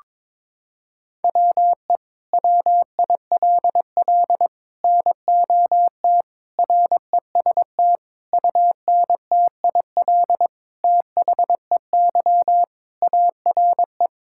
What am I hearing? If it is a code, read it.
Morse code, heard as a keyed tone, about 22 wpm, 700 Hz. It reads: WE WILL NOT REST UNTIL THEY ARE
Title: Sentences from Top 1600 Words 22wpm